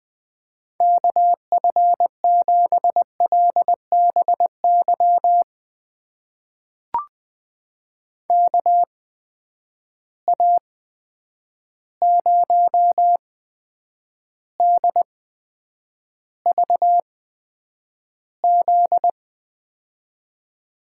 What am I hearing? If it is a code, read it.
Morse code, heard as a keyed tone, about 21 wpm, spaced as for 9 wpm, 700 Hz. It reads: KF7LBY KA0DVZ
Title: USA Call Signs - 6 Characters - 8x Character Spacing - 20wpm